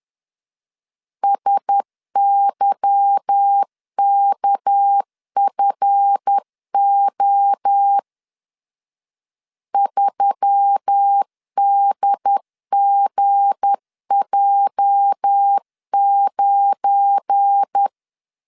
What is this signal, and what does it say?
Morse code, heard as a keyed tone, about 10 wpm, 800 Hz. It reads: SYKFO 3DGJ9
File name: LEZIONI / 8 YJ45 / 8 YJ 10wpm 2651